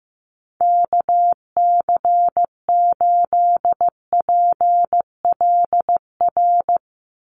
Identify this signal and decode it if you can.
Morse code, heard as a keyed tone, about 15 wpm, 700 Hz. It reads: KC8PLR
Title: USA Call Signs 15wpm